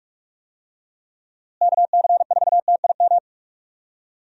Morse code, heard as a keyed tone, about 44 wpm, 700 Hz. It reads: KC4TIM
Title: USA Call Signs - 6 Characters - Repeated 3x - 45wpm